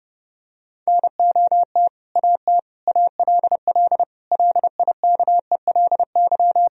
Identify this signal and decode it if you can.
Morse code, heard as a keyed tone, about 30 wpm, 700 Hz. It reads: NOT AT ALL LIKELY
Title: Sentences from Top 1000 Words - Review Part 4 - 30wpm